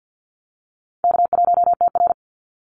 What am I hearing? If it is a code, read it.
Morse code, heard as a keyed tone, about 50 wpm, 700 Hz. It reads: K1TR
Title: Easy Call Signs - 50wpm